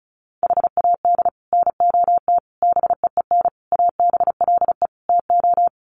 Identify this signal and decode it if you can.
Morse code, heard as a keyed tone, about 34 wpm, 700 Hz. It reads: HAD NOT BEEN ABLE TO
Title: Sets of 5 Words Encore 35wpm